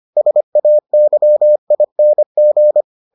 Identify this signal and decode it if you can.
Morse code, heard as a keyed tone, about 25 wpm, 600 Hz. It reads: SAYING